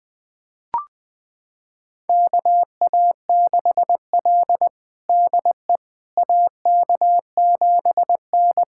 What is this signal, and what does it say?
Morse code, heard as a keyed tone, about 20 wpm, 700 Hz. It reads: KA6L DE AK7N
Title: USA Callsign DE Callsign - 4 Characters - 20wpm